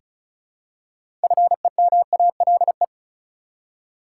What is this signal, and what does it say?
Morse code, heard as a keyed tone, about 36 wpm, 700 Hz. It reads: FEMALE
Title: Six Letter Words - 35wpm